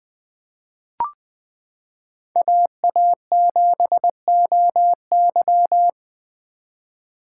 Morse code, heard as a keyed tone, about 21 wpm, 700 Hz. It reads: AA7OY
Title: 2023 SOTA Chaser Call Signs - 20wpm